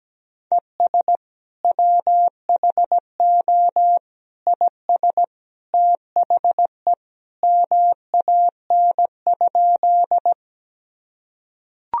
Morse code, heard as a keyed tone, about 17 wpm, 700 Hz. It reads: ES WHO IS THE MAN?
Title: Sentences from Top 200 Words and Abbreviations 17wpm